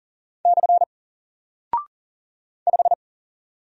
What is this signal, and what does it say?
Morse code, heard as a keyed tone, about 41 wpm, 700 Hz. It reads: / 5
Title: Single Character (Letter:Number:Punctuation:Prosign) 40wpm